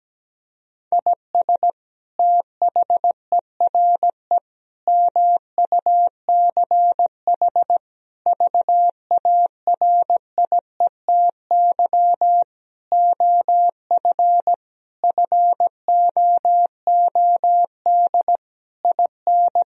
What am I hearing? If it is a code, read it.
Morse code, heard as a keyed tone, about 17 wpm, 700 Hz. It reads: IS THERE MUCH VARIETY OF FOOD IN
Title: Sentences from Top 1000 Words - Review Part 4 - 17wpm